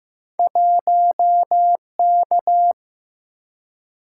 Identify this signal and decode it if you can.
Morse code, heard as a keyed tone, about 15 wpm, 700 Hz. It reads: 1K
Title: Single-Letter Single-Number 15wpm